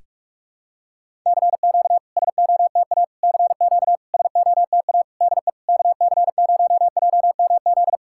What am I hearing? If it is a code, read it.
Morse code, heard as a keyed tone, about 46 wpm, 690 Hz. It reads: CQ SOTA CQ SOTA DE KC0JMZ
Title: SOTA QSO Practice 45wpm